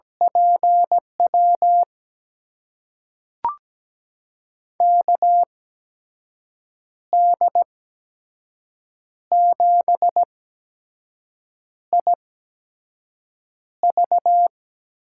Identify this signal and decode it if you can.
Morse code, heard as a keyed tone, about 17 wpm, spaced as for 5 wpm, 700 Hz. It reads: PW KD7IV
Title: USA Call Signs - 6 Characters - 8x Character Spacing - 17wpm